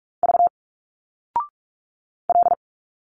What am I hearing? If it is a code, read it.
Morse code, heard as a keyed tone, about 46 wpm, 700 Hz. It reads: V L
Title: Single Letters 45wpm